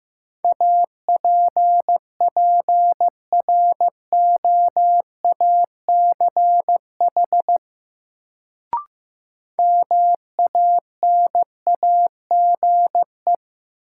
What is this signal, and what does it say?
Morse code, heard as a keyed tone, about 15 wpm, 700 Hz. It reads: APPROACH MANAGE
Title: Sentences from Top 1400 Words - New Word Drill Alt - 15wpm